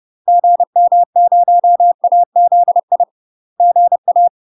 Morse code, heard as a keyed tone, about 30 wpm, 700 Hz. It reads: GM0AZI GA
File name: UK MASTER QSOs TRAINING / 30WPM / UK MASTER QSOs 0028